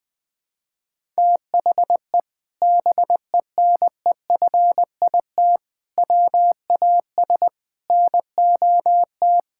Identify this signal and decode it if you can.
Morse code, heard as a keyed tone, about 20 wpm, 700 Hz. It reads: THE BENEFIT WAS NOT
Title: Sentences from Top 1000 Words - Review Part 6 - 20wpm